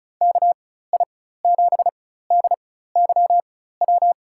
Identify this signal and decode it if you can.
Morse code, heard as a keyed tone, about 36 wpm, spaced as for 16 wpm, 700 Hz. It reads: KI7DYW
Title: USA Call Signs - 4x Character Spacing - 35wpm